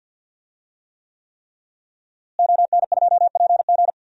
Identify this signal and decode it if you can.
Morse code, heard as a keyed tone, about 52 wpm, 700 Hz. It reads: ON2PG
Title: DXCC-100 Call Signs - 5 Characters - Repeated 3x - 50wpm